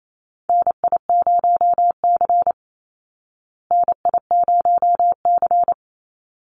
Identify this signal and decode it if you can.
Morse code, heard as a keyed tone, about 28 wpm, 700 Hz. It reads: NI0C NI0C
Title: Call Signs Repeated 3x - 28wpm